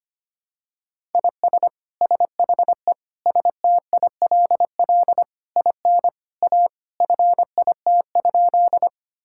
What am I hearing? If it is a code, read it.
Morse code, heard as a keyed tone, about 25 wpm, 700 Hz. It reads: IS SHE STILL IN A FIT?
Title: Sentences from Top 1000 Words - Review Part 5 - 25wpm